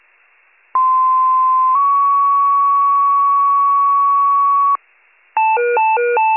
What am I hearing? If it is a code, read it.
Two-tone page, heard as a one-second tone, then a three-second tone, about 1040 then 1120 Hz. It is heard from 0.7 to 4.8 s.